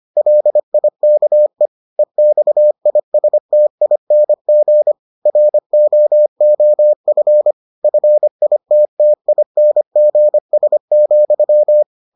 Morse code, heard as a keyed tone, about 25 wpm, 600 Hz. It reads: LIKE EXISTING ROOF FITTINGS,